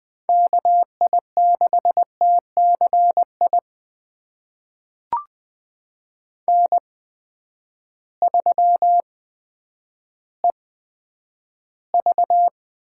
Morse code, heard as a keyed tone, about 20 wpm, spaced as for 10 wpm, 700 Hz. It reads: KI6TCI N3EV
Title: USA Call Signs - 8x Character Spacing - 20wpm